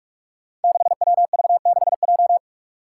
Morse code, heard as a keyed tone, about 45 wpm, 700 Hz. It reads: 6WV6J
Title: Five Challenging Letter-Number Combinations 45wpm